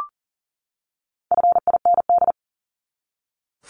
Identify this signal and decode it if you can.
Morse code, heard as a keyed tone, about 40 wpm, 700 Hz. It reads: FIND
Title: ING Suffix - Top 100 Words with Base Words 40wpm